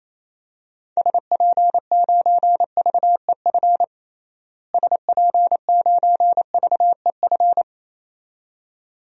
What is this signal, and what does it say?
Morse code, heard as a keyed tone, about 28 wpm, 700 Hz. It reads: SP9VEF SP9VEF
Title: DXCC-100 Call Signs - 6 Characters - Repeated 2x - 28wpm